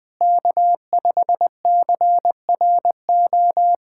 Morse code, heard as a keyed tone, about 20 wpm, 700 Hz. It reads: K5CRO
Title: USA Callsign DE Callsign 20wpm